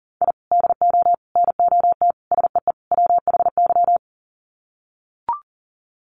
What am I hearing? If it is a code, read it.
Morse code, heard as a keyed tone, about 40 wpm, 700 Hz. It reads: I DO NOT SEE WHY
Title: Sets of 5 Words Encore 40wpm